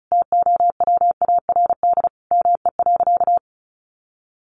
Morse code, heard as a keyed tone, about 34 wpm, 700 Hz. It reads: TOWARD ME.
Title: Sentences from Top 1300 Words 35wpm